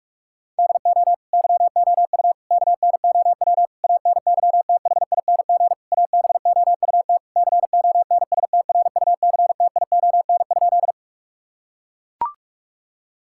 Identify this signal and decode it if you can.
Morse code, heard as a keyed tone, about 41 wpm, 700 Hz. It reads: DO YOU KNOW ANYTHING ABOUT CONSTRUCTION?
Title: Sentences from Top 1600 Words 45wpm